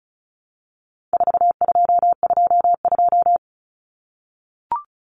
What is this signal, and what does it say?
Morse code, heard as a keyed tone, about 34 wpm, 700 Hz. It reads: 4222
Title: Common Cut Numbers - 4-Digit 35wpm